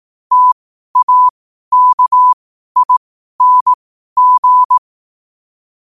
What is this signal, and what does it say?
Morse code, heard as a keyed tone, about 18 wpm, spaced as for 12 wpm, 1000 Hz. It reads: TAKING
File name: qotdfiles / 12wpm